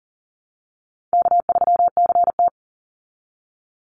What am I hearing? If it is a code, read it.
Morse code, heard as a keyed tone, about 40 wpm, 700 Hz. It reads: K3CT
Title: Call Signs Repeated 2x - 40wpm